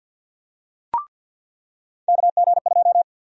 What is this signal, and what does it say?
Morse code, heard as a keyed tone, about 53 wpm, 700 Hz. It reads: KG2
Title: USA Call Sign Prefixes - 50wpm